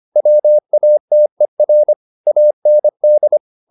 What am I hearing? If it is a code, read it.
Morse code, heard as a keyed tone, about 25 wpm, 600 Hz. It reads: WATER AND